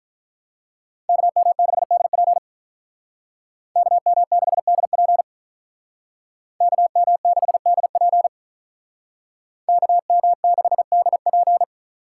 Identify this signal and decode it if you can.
Morse code, heard as a keyed tone, about 43 wpm, 700 Hz. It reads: KM6DP KM6DP KM6DP KM6DP
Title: Speed Racing - USA 2x2 Call Signs 35wpm